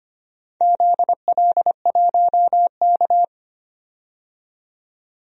Morse code, heard as a keyed tone, about 25 wpm, 700 Hz. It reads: ZL1K
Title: DXCC-100 Call Signs - 4 Characters - 25wpm